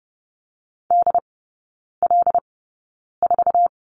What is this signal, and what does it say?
Morse code, heard as a keyed tone, about 30 wpm, spaced as for 9 wpm, 690 Hz. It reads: DL4
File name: Lesson 17e - Letter u - Three Characters 30wpm